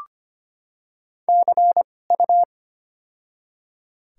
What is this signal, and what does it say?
Morse code, heard as a keyed tone, about 25 wpm, spaced as for 18 wpm, 700 Hz. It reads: CU
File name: Top 90 QSO Elements - 2x Character Spacing - 25wpm